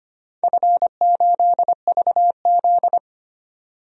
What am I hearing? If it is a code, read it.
Morse code, heard as a keyed tone, about 25 wpm, 700 Hz. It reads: F8VZ
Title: DXCC-100 Call Signs - 4 Characters - 25wpm